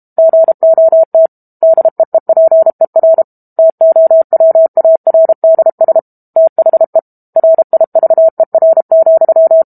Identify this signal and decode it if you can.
Morse code, heard as a keyed tone, about 32 wpm, 650 Hz. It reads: GOT DEEPER TOWARDS THE RIVER,
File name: Book Two / 06-Chapter 6 - 32 WPM